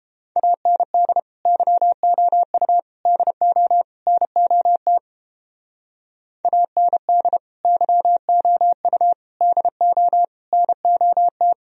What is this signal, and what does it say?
Morse code, heard as a keyed tone, about 32 wpm, 700 Hz. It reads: AND YOU DO NOT AND YOU DO NOT